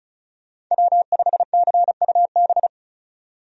Morse code, heard as a keyed tone, about 36 wpm, 700 Hz. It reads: W5CUB
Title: USA Call Signs - 5 Characters - 35wpm